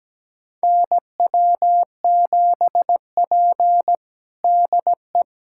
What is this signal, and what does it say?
Morse code, heard as a keyed tone, about 17 wpm, 700 Hz. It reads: NW7P DE
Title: USA Callsign DE Callsign - 4 Characters - 17wpm